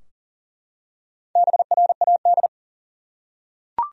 Morse code, heard as a keyed tone, about 40 wpm, 700 Hz. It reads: BRAD
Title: USA Common Names - Max 4 Letters - 40wpm